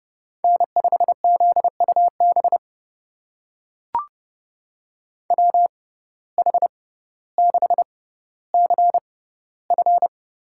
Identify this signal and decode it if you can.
Morse code, heard as a keyed tone, about 30 wpm, spaced as for 15 wpm, 700 Hz. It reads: N5ZUB WH6CF
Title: USA Call Signs - 5 Characters - 6x Character Spacing - 30wpm